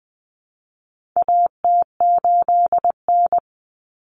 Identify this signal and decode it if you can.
Morse code, heard as a keyed tone, about 20 wpm, 700 Hz. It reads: AT8N
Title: Common Cut Numbers - 4-Digit 20wpm